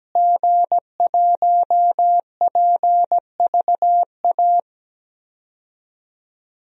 Morse code, heard as a keyed tone, about 17 wpm, 700 Hz. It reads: G1PVA
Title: DXCC-100 Call Signs - 5 Characters - Repeated 2x - 17wpm